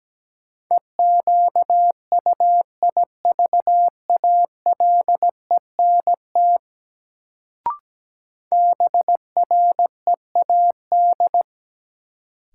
Morse code, heard as a keyed tone, about 17 wpm, 700 Hz. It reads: EQUIVALENT BREAD
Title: Sentences from Top 1600 Words - New Word Drill - 17wpm